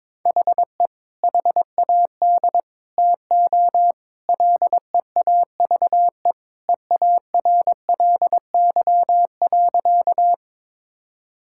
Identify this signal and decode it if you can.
Morse code, heard as a keyed tone, about 22 wpm, 700 Hz. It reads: HE HAD TO LEAVE EARLY.
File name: practice / 22wpm / Sentences with 2-Word Phrases - 22wpm